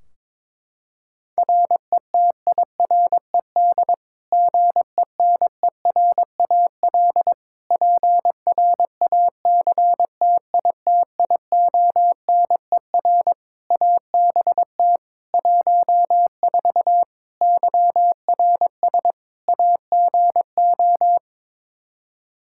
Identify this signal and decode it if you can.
Morse code, heard as a keyed tone, about 22 wpm, 700 Hz. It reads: RETIRED GENERAL PRACTITIONER ABT 14 YRS AGO
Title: Practice QSO - Exchange 3 - More Info - 22wpm